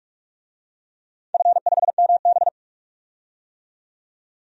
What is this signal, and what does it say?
Morse code, heard as a keyed tone, about 44 wpm, 700 Hz. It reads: F5MB
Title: DXCC-100 Call Signs - 4 Characters - 45wpm